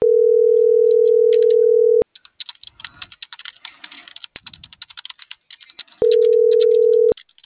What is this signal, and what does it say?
Ringback tone (North America), 440 + 480 Hz.